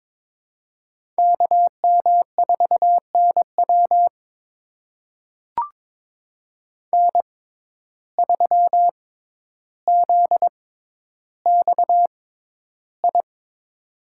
Morse code, heard as a keyed tone, about 22 wpm, spaced as for 11 wpm, 700 Hz. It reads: KM4NW N3ZXI
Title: USA Call Signs - 5 Characters - 6x Character Spacing - 22wpm